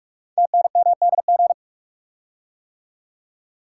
Morse code, heard as a keyed tone, about 44 wpm, 700 Hz. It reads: TNMDG
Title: Code Groups - 5 Letters - 45wpm